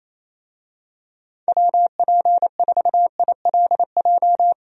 Morse code, heard as a keyed tone, about 28 wpm, 700 Hz. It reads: WP4ILJ
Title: USA Call Signs - 6 Characters - 28wpm